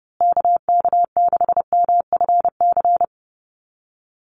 Morse code, heard as a keyed tone, about 30 wpm, 700 Hz. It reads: KK6MFC
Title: USA Call Signs Repeated 2x 30wpm